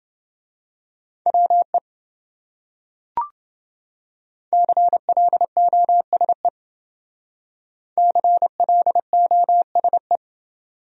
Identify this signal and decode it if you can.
Morse code, heard as a keyed tone, about 28 wpm, 700 Hz. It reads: WE CLOSE CLOSE